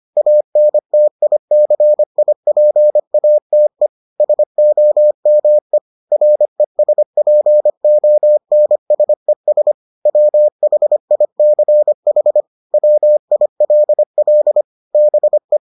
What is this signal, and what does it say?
Morse code, heard as a keyed tone, about 25 wpm, 600 Hz. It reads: ANTICIPATE SOME RESPONSES WHICH WILL BE